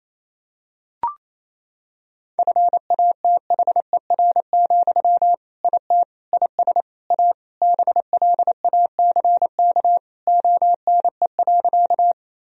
Morse code, heard as a keyed tone, about 28 wpm, 700 Hz. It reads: FATHER, IT IS A BLACK ONE.
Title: Sentences from Top 1000 Words - Review Part 2 - 28wpm